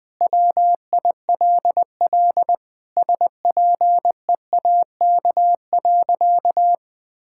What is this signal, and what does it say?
Morse code, heard as a keyed tone, about 20 wpm, 700 Hz. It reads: WILL SPEAK.
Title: Sentences with 2-Word Phrases - 20wpm